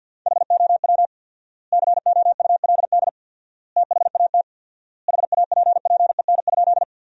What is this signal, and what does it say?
Morse code, heard as a keyed tone, about 47 wpm, 700 Hz. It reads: HOW COULD THAT HAPPEN?
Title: 4x Word-Spacing - Sentences from Top 400 Words - 50wpm